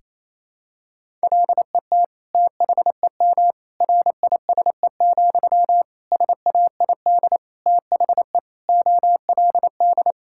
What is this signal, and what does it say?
Morse code, heard as a keyed tone, about 28 wpm, 700 Hz. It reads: LET THEM RISE, SAID THE OLD
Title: Sentences from Top 1000 Words - Review Part 6 - 28wpm